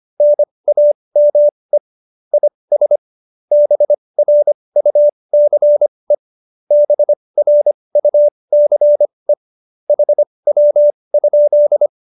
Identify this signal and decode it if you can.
Morse code, heard as a keyed tone, about 25 wpm, spaced as for 20 wpm, 590 Hz. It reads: NAME IS BRUCE BRUCE HW?